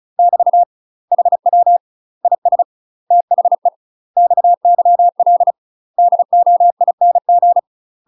Morse code, heard as a keyed tone, about 34 wpm, 700 Hz. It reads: <BT> HW IS THE XYL DOING
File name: UK MASTER QSOs TRAINING / 35WPM / UK MASTER QSOs 0042